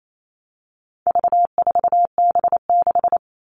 Morse code, heard as a keyed tone, about 28 wpm, 700 Hz. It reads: V4B6
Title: All Cut Numbers - 4-Digit 28wpm